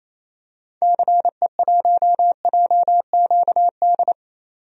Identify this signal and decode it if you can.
Morse code, heard as a keyed tone, about 28 wpm, 700 Hz. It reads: CE1JQD
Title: DXCC-100 Call Signs - 6 Characters - 28wpm